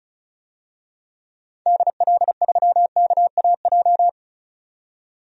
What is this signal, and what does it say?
Morse code, heard as a keyed tone, about 36 wpm, 700 Hz. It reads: DL3KAJ
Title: DXCC-100 Call Signs - 6 Characters - 35wpm